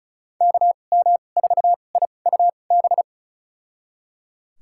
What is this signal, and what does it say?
Morse code, heard as a keyed tone, about 34 wpm, spaced as for 25 wpm, 700 Hz. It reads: KM4IUB
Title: USA Call Signs - 6 Characters - 2x Character Spacing - 35wpm